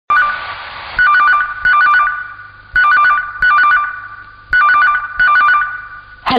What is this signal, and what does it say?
There is no code.